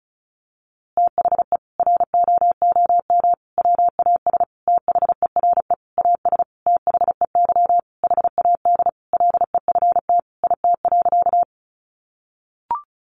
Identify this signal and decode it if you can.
Morse code, heard as a keyed tone, about 34 wpm, 700 Hz. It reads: THE ROOM WAS THERE AS THEY HAD LEFT IT.
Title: A Sound Of Thunder 35wpm